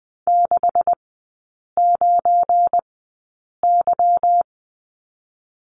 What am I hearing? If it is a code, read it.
Morse code, heard as a keyed tone, about 20 wpm, spaced as for 8 wpm, 700 Hz. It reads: 69Y
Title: 205 Lesson 33e - Letter j - Three Characters 20wpm